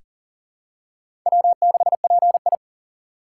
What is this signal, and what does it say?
Morse code, heard as a keyed tone, about 40 wpm, 700 Hz. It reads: W6PI
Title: USA Call Signs - 4 Characters - 2x Character Spacing - 40wpm